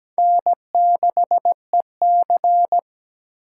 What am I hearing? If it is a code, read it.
Morse code, heard as a keyed tone, about 20 wpm, 700 Hz. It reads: N6EC
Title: Speed Racing - USA 1x2 Call Signs 17wpm